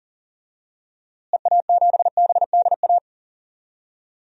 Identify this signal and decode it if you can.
Morse code, heard as a keyed tone, about 40 wpm, 700 Hz. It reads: EA7BDA
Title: DXCC-100 Call Signs - 6 Characters - Repeated 2x - 40wpm